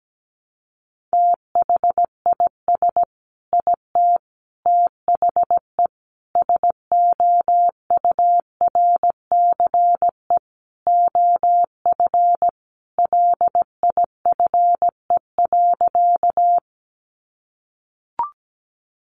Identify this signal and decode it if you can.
Morse code, heard as a keyed tone, about 17 wpm, 700 Hz. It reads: THIS IT THE SOURCE OF LIFE.
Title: Sentences from Top 800 Words 17wpm